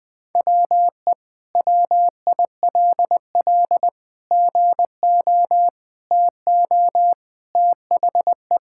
Morse code, heard as a keyed tone, about 21 wpm, 700 Hz. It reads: WE WILL GO TO THE
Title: Sentences with 2-Word Phrases - Spoken Beginning - 20wpm